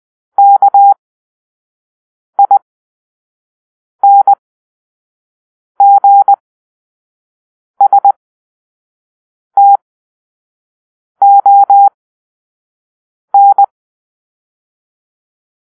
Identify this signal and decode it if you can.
Morse code, heard as a keyed tone, about 20 wpm, spaced as for 5 wpm, 800 Hz. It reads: KINGSTON
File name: arrl / Grp-7 Drill-3